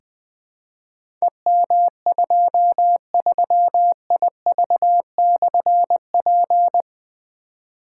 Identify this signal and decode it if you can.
Morse code, heard as a keyed tone, about 20 wpm, 700 Hz. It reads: EM23IV/P
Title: Difficult Call Signs 20wpm